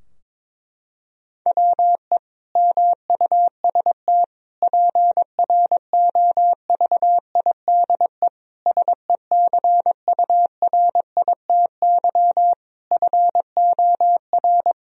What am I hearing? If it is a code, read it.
Morse code, heard as a keyed tone, about 22 wpm, 700 Hz. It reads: WE MUST PROVIDE SECURITY FOR
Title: Sentences from Top 1000 Words - Review Part 4 - 22wpm